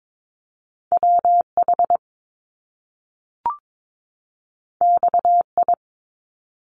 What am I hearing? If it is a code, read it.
Morse code, heard as a keyed tone, about 22 wpm, 700 Hz. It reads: WH XI
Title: Top Two Letter Combinations in Words 22wpm